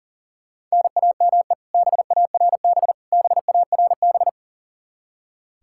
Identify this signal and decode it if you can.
Morse code, heard as a keyed tone, about 40 wpm, 700 Hz. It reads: NAME BARB BARB
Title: Practice QSO - Exchange 1 - Beginning - 40wpm